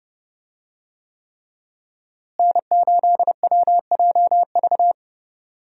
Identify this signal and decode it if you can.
Morse code, heard as a keyed tone, about 32 wpm, 700 Hz. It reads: N8WJV
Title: USA Call Signs - 5 Characters - Repeated 3x - 30wpm